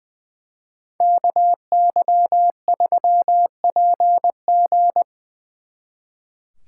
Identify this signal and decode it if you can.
Morse code, heard as a keyed tone, about 23 wpm, 700 Hz. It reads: KY3PG
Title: Speed Racing - USA 2x2 Call Signs 20wpm